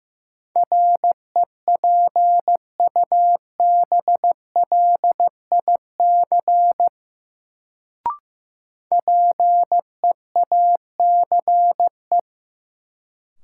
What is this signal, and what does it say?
Morse code, heard as a keyed tone, about 15 wpm, 700 Hz. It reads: REPUBLIC PEACE
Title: Sentences from Top 1500 Words - New Word Drill - 15wpm